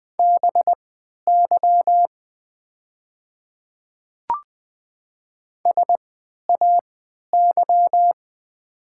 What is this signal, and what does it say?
Morse code, heard as a keyed tone, about 20 wpm, spaced as for 11 wpm, 690 Hz. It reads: BY SAY
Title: 3x Character Spacing - Top 300 Words - 20wpm